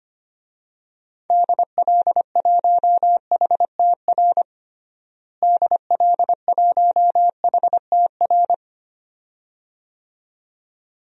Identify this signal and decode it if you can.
Morse code, heard as a keyed tone, about 25 wpm, 700 Hz. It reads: DL1HTR DL1HTR
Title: DXCC-100 Call Signs - 6 Characters - Repeated 2x - 25wpm